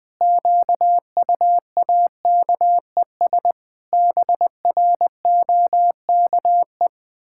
Morse code, heard as a keyed tone, about 21 wpm, 700 Hz. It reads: QUAKES BROKE
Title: Pangram Challenge v2 - 20wpm